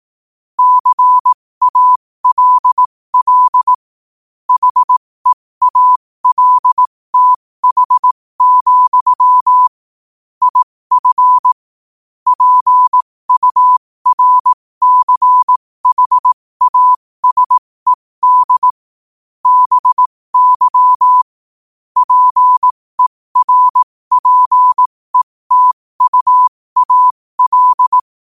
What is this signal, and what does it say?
Morse code, heard as a keyed tone, about 18 wpm, spaced as for 15 wpm, 1000 Hz. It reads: CALL HEALTH, IF PURCHASED BY PERPETUAL
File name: qotdfiles / 15wpm